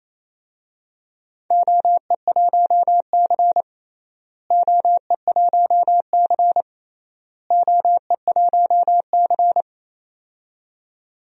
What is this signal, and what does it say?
Morse code, heard as a keyed tone, about 28 wpm, 700 Hz. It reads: OE1C OE1C OE1C
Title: DXCC-100 Call Signs - 4 Characters - Repeated 3x - 28wpm